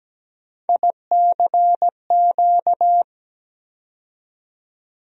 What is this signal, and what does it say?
Morse code, heard as a keyed tone, about 17 wpm, 700 Hz. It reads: ICQ
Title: USA Call Sign Postfixes - 3 Letters 17wpm